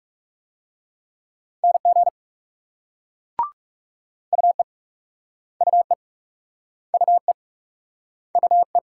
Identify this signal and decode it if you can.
Morse code, heard as a keyed tone, about 42 wpm, 700 Hz. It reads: NG UE UE UE UE